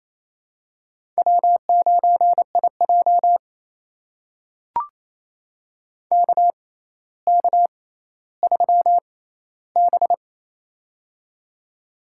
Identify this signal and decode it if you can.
Morse code, heard as a keyed tone, about 27 wpm, spaced as for 14 wpm, 700 Hz. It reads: W9IJ KK3B
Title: USA Call Signs - 4 Characters - 6x Character Spacing - 28wpm